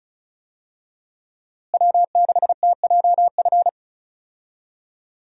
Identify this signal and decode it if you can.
Morse code, heard as a keyed tone, about 35 wpm, 700 Hz. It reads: W6TJF
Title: USA Call Signs - 5 Characters - Repeated 3x - 35wpm